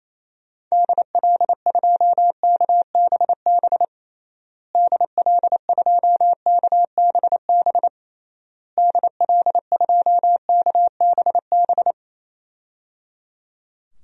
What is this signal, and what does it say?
Morse code, heard as a keyed tone, about 28 wpm, 700 Hz. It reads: DL2KBB DL2KBB DL2KBB
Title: DXCC-100 Call Signs - 6 Characters - Repeated 3x - 28wpm